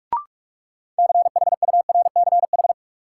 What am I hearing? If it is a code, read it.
Morse code, heard as a keyed tone, about 43 wpm, 700 Hz. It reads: CHURCH
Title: Sets of 5 Words - New Word Drill - 45wpm